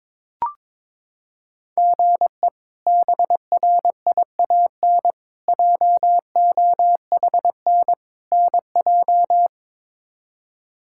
Morse code, heard as a keyed tone, about 22 wpm, 700 Hz. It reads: GE BRIAN JOHN NJ
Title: K1USN SST QSO Practice - 22wpm